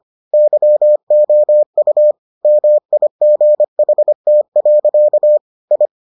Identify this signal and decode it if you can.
Morse code, heard as a keyed tone, about 25 wpm, 600 Hz. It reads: YOU MIGHT. I